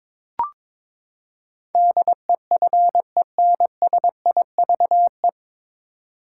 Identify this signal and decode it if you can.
Morse code, heard as a keyed tone, about 22 wpm, 700 Hz. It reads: DEFENSIVE
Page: Morse Code Ninja practice 22wpm